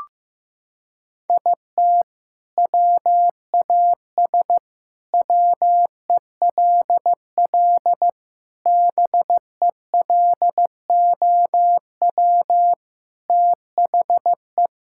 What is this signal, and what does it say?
Morse code, heard as a keyed tone, about 15 wpm, 700 Hz. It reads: IT WAS WELL BELOW THE
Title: Sentences from Top 1000 Words - Review Part 6 - 15wpm